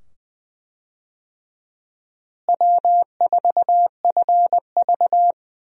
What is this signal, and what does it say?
Morse code, heard as a keyed tone, about 20 wpm, 700 Hz. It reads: W4FV
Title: USA Call Signs - 4 Characters - Repeated 2x - 20wpm